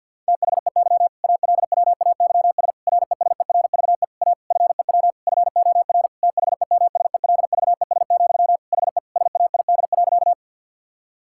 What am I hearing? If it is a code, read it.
Morse code, heard as a keyed tone, about 51 wpm, 700 Hz. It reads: THEY ALWAYS RESERVE A FEW FOR THEMSELVES, HE SAID.
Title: Sentences from Top 1000 Words - Review Part 6 - 50wpm